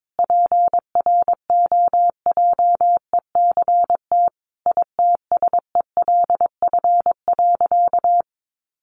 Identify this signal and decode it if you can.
Morse code, heard as a keyed tone, about 22 wpm, 700 Hz. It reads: PROJECT ITSELF.
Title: Sentences from Top 1400 Words 22wpm